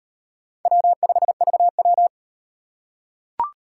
Morse code, heard as a keyed tone, about 38 wpm, 700 Hz. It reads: W5VW